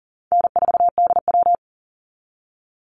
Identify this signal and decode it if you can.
Morse code, heard as a keyed tone, about 40 wpm, 700 Hz. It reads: N4DW
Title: Call Signs Repeated 3x - 40wpm